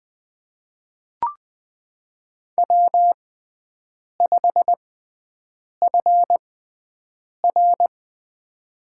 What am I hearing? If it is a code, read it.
Morse code, heard as a keyed tone, about 20 wpm, spaced as for 7 wpm, 700 Hz. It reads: W5FR
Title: USA Call Signs - 6x Character Spacing - 20wpm